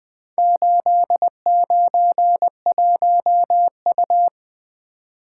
Morse code, heard as a keyed tone, about 20 wpm, 700 Hz. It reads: 891U
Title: Four Letter-Number Combinations - 20wpm